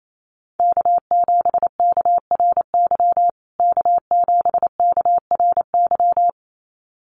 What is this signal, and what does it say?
Morse code, heard as a keyed tone, about 28 wpm, 700 Hz. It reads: K7KRY K7KRY
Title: USA Call Signs Repeated 2x 28wpm